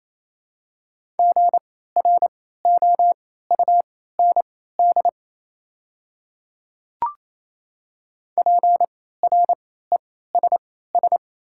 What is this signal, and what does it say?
Morse code, heard as a keyed tone, about 28 wpm, spaced as for 16 wpm, 690 Hz. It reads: GROUND PRESS
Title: 3x Character Spacing - Top 500 Words - 28wpm